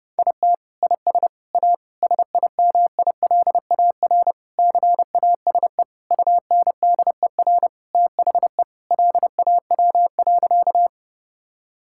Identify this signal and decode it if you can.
Morse code, heard as a keyed tone, about 32 wpm, 700 Hz. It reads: IT IS A SIMILAR CASE UNDER THE LAW.
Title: Sentences from Top 1000 Words - Review Part 5 - 30wpm